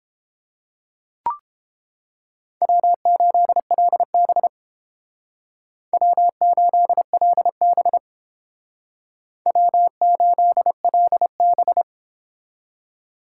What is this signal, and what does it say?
Morse code, heard as a keyed tone, about 30 wpm, 700 Hz. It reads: W8LB W8LB W8LB